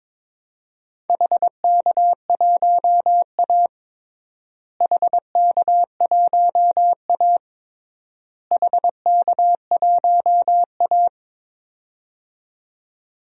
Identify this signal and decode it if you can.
Morse code, heard as a keyed tone, about 22 wpm, 700 Hz. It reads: HK1A HK1A HK1A
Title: DXCC-100 Call Signs - 4 Characters - Repeated 3x - 22wpm